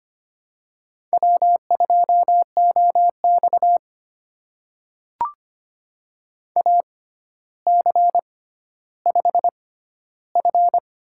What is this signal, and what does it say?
Morse code, heard as a keyed tone, about 25 wpm, spaced as for 13 wpm, 700 Hz. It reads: W2OX AC5F
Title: USA Call Signs - 4 Characters - 6x Character Spacing - 25wpm